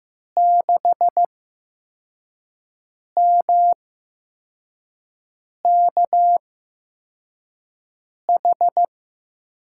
Morse code, heard as a keyed tone, about 15 wpm, spaced as for 4 wpm, 700 Hz. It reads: 6MKH
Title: USA Call Signs - 6 Characters - 8x Character Spacing - 15wpm